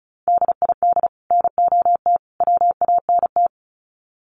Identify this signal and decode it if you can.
Morse code, heard as a keyed tone, about 34 wpm, 700 Hz. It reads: DID NOT WANT
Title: Sets of 3 Words 35wpm